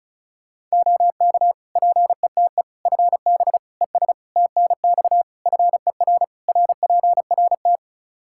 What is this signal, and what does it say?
Morse code, heard as a keyed tone, about 36 wpm, 700 Hz. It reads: OK PETE FB ES TNX FER RPRT
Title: Practice QSO - Exchange 2 - Info - 35wpm